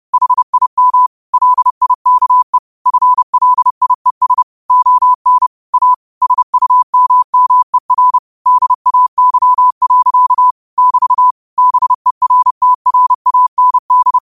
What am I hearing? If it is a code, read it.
Morse code, heard as a keyed tone, about 30 wpm, 1000 Hz. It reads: HIM LIKE FLIES ON A SUMMER DAY. <BT> BERTRAND
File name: qotdfiles / 30wpm